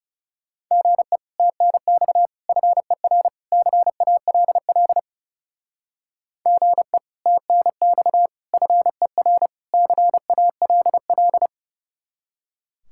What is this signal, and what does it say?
Morse code, heard as a keyed tone, about 35 wpm, 700 Hz. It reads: GE TNX FER CALL GE TNX FER CALL
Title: Speed-Racing Common QSO Phrases Encore - 30wpm